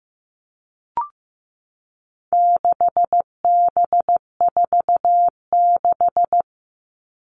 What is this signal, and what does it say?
Morse code, heard as a keyed tone, about 15 wpm, 700 Hz. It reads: 6B46
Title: All Cut Numbers - 4-Digit 15wpm